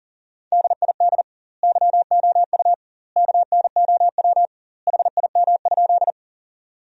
Morse code, heard as a keyed tone, about 40 wpm, 700 Hz. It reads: DID YOU KNOW HIM?
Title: 2x Word-Spacing - Sentences from Top 100 Words - 40wpm